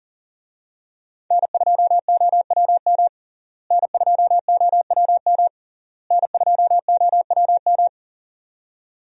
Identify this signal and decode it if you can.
Morse code, heard as a keyed tone, about 40 wpm, 700 Hz. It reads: N2OWM N2OWM N2OWM
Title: USA Call Signs - 5 Characters - Repeated 3x - 40wpm